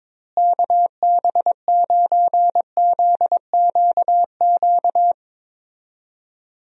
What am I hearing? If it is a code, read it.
Morse code, heard as a keyed tone, about 22 wpm, 700 Hz. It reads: KB9ZQQ
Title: USA Call Signs - 6 Characters - Repeated 2x - 22wpm